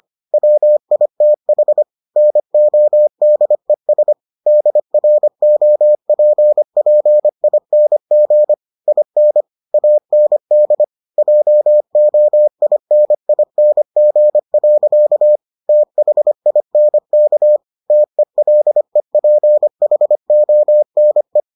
Morse code, heard as a keyed tone, about 25 wpm, 600 Hz. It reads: WITH NODES DROPPING IN AND JOINING. THINK TELEPHONE